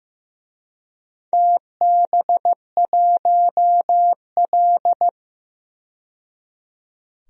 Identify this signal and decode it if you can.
Morse code, heard as a keyed tone, about 15 wpm, 700 Hz. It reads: TB1L
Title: DXCC-100 Call Signs - 4 Characters - 15wpm